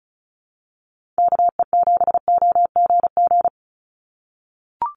Morse code, heard as a keyed tone, about 36 wpm, 700 Hz. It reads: KE7OGG
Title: USA Call Signs 35wpm